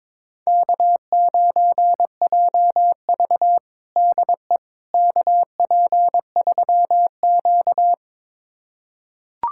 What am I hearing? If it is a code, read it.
Morse code, heard as a keyed tone, about 22 wpm, 700 Hz. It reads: K9JV DE KP3Q
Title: USA Callsign DE Callsign - 4 Characters - 22wpm